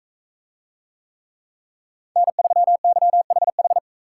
Morse code, heard as a keyed tone, about 42 wpm, 700 Hz. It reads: N3YHH